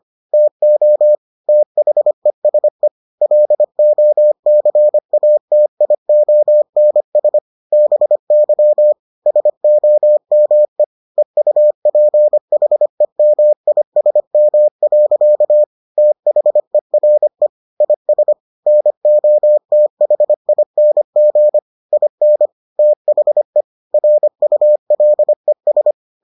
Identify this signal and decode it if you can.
Morse code, heard as a keyed tone, about 25 wpm, 600 Hz. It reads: TO THESE LOCATIONS BY SOME EUPHEMISM. THERE IS NOTHING IN THE RULES